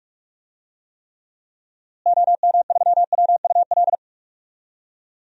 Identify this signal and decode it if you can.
Morse code, heard as a keyed tone, about 46 wpm, 700 Hz. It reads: OM3WUL
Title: DXCC-100 Call Signs - 6 Characters - Repeated 3x - 45wpm